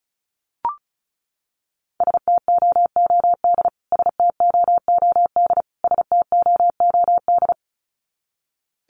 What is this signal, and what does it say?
Morse code, heard as a keyed tone, about 34 wpm, 700 Hz. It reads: STOOD STOOD STOOD
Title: Top 500 Words Repeated 35wpm